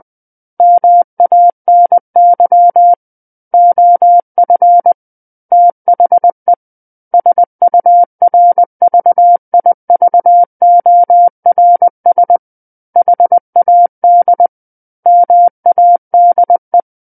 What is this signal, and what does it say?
Morse code, heard as a keyed tone, about 20 wpm, 700 Hz. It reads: MANY OF THE SURVIVORS HAD MADE
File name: 20 WPM